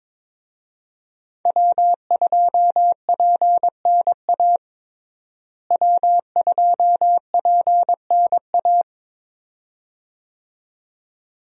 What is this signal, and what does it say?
Morse code, heard as a keyed tone, about 22 wpm, 700 Hz. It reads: W2PNA W2PNA
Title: USA Call Signs - 5 Characters - Repeated 2x - 22wpm